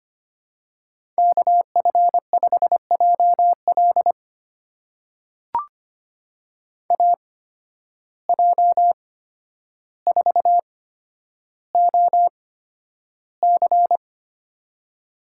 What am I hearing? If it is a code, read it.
Morse code, heard as a keyed tone, about 25 wpm, spaced as for 11 wpm, 700 Hz. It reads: KF5JL AJ4OC
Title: USA Call Signs - 5 Characters - 8x Character Spacing - 25wpm